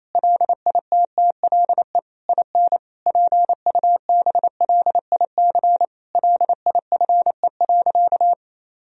Morse code, heard as a keyed tone, about 28 wpm, 700 Hz. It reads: LITTLE IN PUBLIC LIFE.
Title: Sentences from Top 1000 Words - Review Part 3 - 28wpm